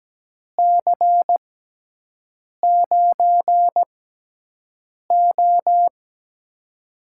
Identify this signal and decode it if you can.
Morse code, heard as a keyed tone, about 17 wpm, spaced as for 6 wpm, 700 Hz. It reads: C9O
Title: USA Call Signs - 4 Characters - 6x Character Spacing - 17wpm